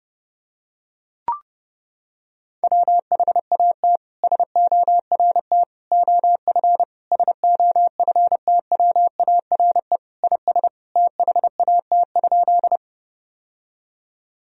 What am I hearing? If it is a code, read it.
Morse code, heard as a keyed tone, about 30 wpm, 700 Hz. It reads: WHAT SORT OF SOFTWARE IS THAT?
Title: Sentences from Top 1000 Words - Review Part 3 - 30wpm